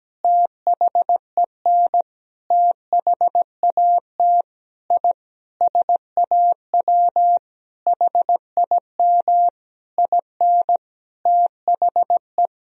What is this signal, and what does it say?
Morse code, heard as a keyed tone, about 17 wpm, 700 Hz. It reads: THEN THAT I SAW HIM IN THE
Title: Sentences from Top 1000 Words - Review Part 3 - 17wpm